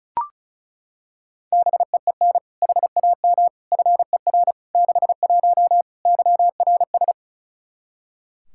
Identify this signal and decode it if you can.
Morse code, heard as a keyed tone, about 36 wpm, 700 Hz. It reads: BEEN HAM FER 61 YRS
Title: Practice QSO - Exchange 3 - More Info - 35wpm